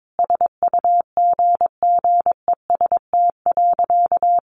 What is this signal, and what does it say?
Morse code, heard as a keyed tone, about 22 wpm, 700 Hz. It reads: SUGGEST.
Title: Sentences from Top 1500 Words 22wpm